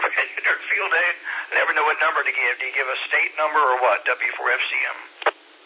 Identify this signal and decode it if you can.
Emergency Alert System (SAME) tones (header bursts), heard as harsh buzzing bursts, with speech. It is heard from 0.3 to 4.6 s.